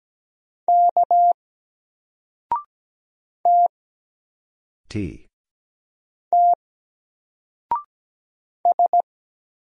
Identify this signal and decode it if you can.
Morse code, heard as a keyed tone, about 17 wpm, 700 Hz. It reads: K T T S
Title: Single Letter-Number Alt 17wpm